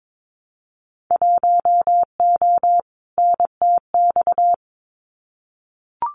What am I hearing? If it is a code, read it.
Morse code, heard as a keyed tone, about 22 wpm, 700 Hz. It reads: 1O NTX
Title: Winter Field Day Exchanges 22wpm